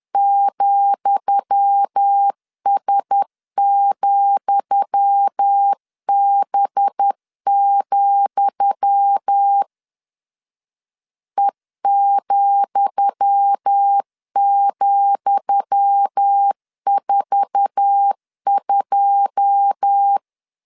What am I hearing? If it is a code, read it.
Morse code, heard as a keyed tone, about 10 wpm, 800 Hz. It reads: ,S,B, E,,42
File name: LEZIONI / 10 SIGNS / 10D COMA 10wpm 3013